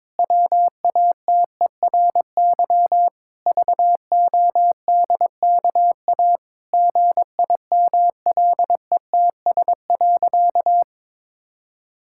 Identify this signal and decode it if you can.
Morse code, heard as a keyed tone, about 22 wpm, 700 Hz. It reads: WATERY VODKA GIMLETS.
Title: Pangram Challenge v2 - 22wpm